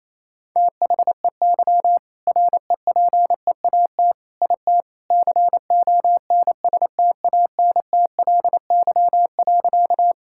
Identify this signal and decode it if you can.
Morse code, heard as a keyed tone, about 28 wpm, 700 Hz. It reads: THEY REPEAT IT CONSTANTLY.
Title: Sentences from Top 1600 Words 28wpm